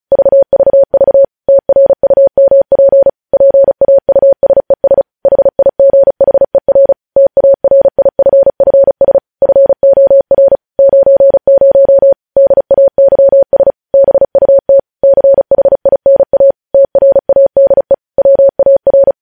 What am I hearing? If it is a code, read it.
Morse code, heard as a keyed tone, about 36 wpm, 550 Hz. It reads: VVV TRUMP PAUSES HIGHER TARIFFS FOR 90 DAYS BUT CHINA TRADE WAR
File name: podcast / News headlines-1x-35x35